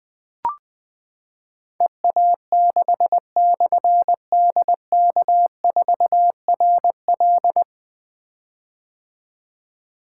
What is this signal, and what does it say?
Morse code, heard as a keyed tone, about 20 wpm, 700 Hz. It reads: EA6/DK4RL
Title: Difficult Call Signs 20wpm